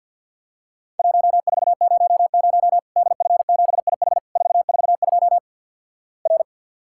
Morse code, heard as a keyed tone, about 50 wpm, 700 Hz. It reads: 1400 DF7IH 442 R
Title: Worked All Europe DX Contest - QTC Exchange - 50wpm